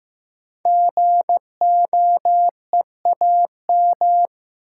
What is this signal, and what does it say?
Morse code, heard as a keyed tone, about 15 wpm, 700 Hz. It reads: GOEAM
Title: Code Groups - 5 Letters - 15wpm